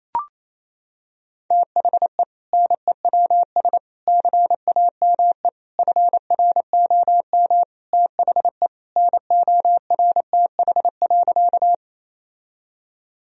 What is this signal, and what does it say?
Morse code, heard as a keyed tone, about 27 wpm, 700 Hz. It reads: THE NEWS CAME FROM THE NORTH.
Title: Sentences from Top 1000 Words - Review Part 1 - 28wpm